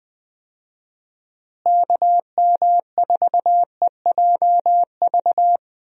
Morse code, heard as a keyed tone, about 21 wpm, 700 Hz. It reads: KM4EJV
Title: USA Call Signs - 6 Characters - Repeated 2x - 20wpm